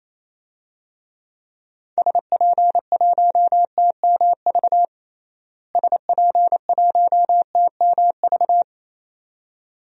Morse code, heard as a keyed tone, about 27 wpm, 700 Hz. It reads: SP1TMV SP1TMV
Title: DXCC-100 Call Signs - 6 Characters - Repeated 2x - 28wpm